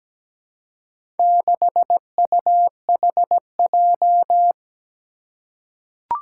Morse code, heard as a keyed tone, about 17 wpm, 700 Hz. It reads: 6UHJ
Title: Four Challenging Letter-Number Combinations 17wpm